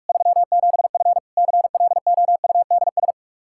Morse code, heard as a keyed tone, about 43 wpm, 700 Hz. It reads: 37F CLOUDS
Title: Practice QSO - Exchange 2 - Info - 45wpm